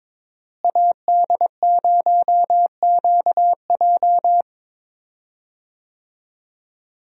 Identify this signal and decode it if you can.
Morse code, heard as a keyed tone, about 22 wpm, 700 Hz. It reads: AD0QJ
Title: USA Call Signs - 5 Characters - Repeated 2x - 22wpm